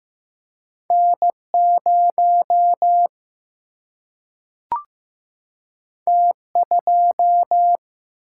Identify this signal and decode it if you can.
Morse code, heard as a keyed tone, about 15 wpm, 700 Hz. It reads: N0 T2
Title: Single-Letter Single-Number 15wpm